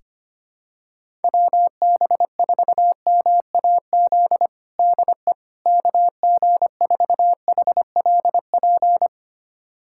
Morse code, heard as a keyed tone, about 25 wpm, 700 Hz. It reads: WB4MAZ DE KG4HLP
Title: USA Callsign DE Callsign - 6 Characters - 25wpm